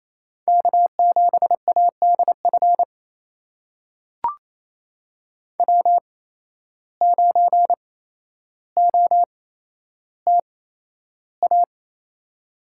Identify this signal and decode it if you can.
Morse code, heard as a keyed tone, about 28 wpm, spaced as for 12 wpm, 700 Hz. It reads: K7ADF W9OTA
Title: USA Call Signs - 5 Characters - 8x Character Spacing - 28wpm